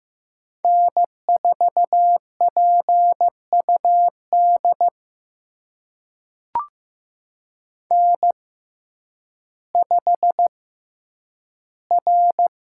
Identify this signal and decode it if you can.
Morse code, heard as a keyed tone, about 15 wpm, spaced as for 9 wpm, 700 Hz. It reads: N4PUD N5R
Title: USA Call Signs - 5 Characters - 6x Character Spacing - 15wpm